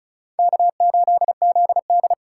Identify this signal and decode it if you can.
Morse code, heard as a keyed tone, about 36 wpm, 700 Hz. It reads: K8ZD
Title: USA Call Signs - 3x Character Spacing - 35wpm